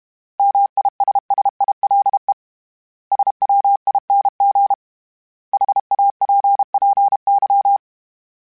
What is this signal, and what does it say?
Morse code, heard as a keyed tone, about 31 wpm, 800 Hz. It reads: MISSILE SWING HAPPY
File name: Misc / Misc-mp3s / Words @30